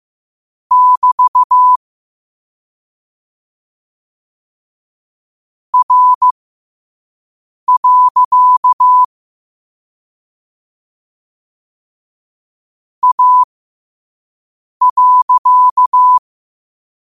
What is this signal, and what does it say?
Morse code, heard as a keyed tone, about 15 wpm, spaced as for 5 wpm, 1000 Hz. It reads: <BT> R. A.
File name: qotdfiles / 5wpm